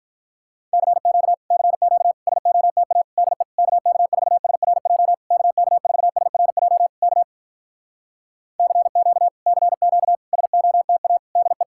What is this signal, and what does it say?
Morse code, heard as a keyed tone, about 52 wpm, 700 Hz. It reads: CQ CQ SOTA DE KK4SRJ KK4SRJ K CQ CQ SOTA DE